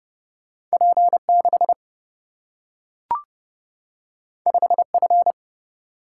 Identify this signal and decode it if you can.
Morse code, heard as a keyed tone, about 30 wpm, 700 Hz. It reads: P6 5F
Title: Single-Letter Single-Number 30wpm